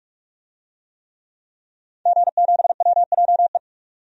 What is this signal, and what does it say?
Morse code, heard as a keyed tone, about 46 wpm, 700 Hz. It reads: G7WJE
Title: DXCC-100 Call Signs - 5 Characters - 45wpm